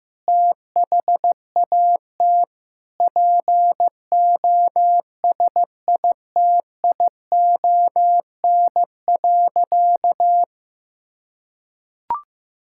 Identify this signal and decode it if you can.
Morse code, heard as a keyed tone, about 15 wpm, 700 Hz. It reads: THAT POSITION.
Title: Sentences from Top 1000 Words - Review Part 5 - 15wpm